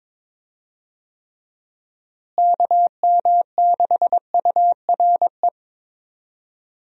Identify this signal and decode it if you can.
Morse code, heard as a keyed tone, about 22 wpm, 700 Hz. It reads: KM6URE
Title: USA Call Signs - 6 Characters - Repeated 3x - 22wpm